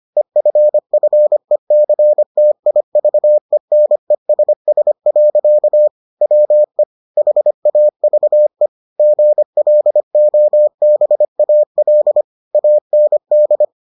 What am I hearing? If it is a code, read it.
Morse code, heard as a keyed tone, about 25 wpm, 600 Hz. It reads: EFFECTIVENESS. WE HAVE GLOBAL AND